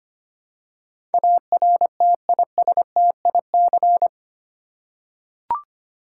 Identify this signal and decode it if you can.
Morse code, heard as a keyed tone, about 25 wpm, 700 Hz. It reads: ARTISTIC
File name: 8 Letter Words - First 3 Letters Given - 25wpm